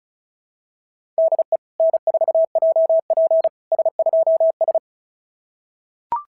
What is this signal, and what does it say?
Morse code, heard as a keyed tone, about 33 wpm, 650 Hz. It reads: DE N4JP S2S
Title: SOTA QSO Practice 35wpm